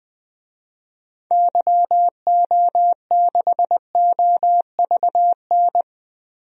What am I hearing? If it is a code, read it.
Morse code, heard as a keyed tone, about 21 wpm, 700 Hz. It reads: YO6OVN
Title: DXCC-100 Call Signs - 6 Characters - Repeated 2x - 20wpm